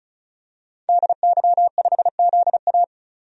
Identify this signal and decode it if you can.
Morse code, heard as a keyed tone, about 36 wpm, 700 Hz. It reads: DY5ZA
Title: DXCC-100 Call Signs - 5 Characters - 35wpm